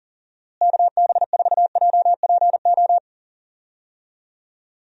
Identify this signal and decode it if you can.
Morse code, heard as a keyed tone, about 38 wpm, 700 Hz. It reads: KB4JPO
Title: USA Call Signs - 6 Characters - Repeated 3x - 40wpm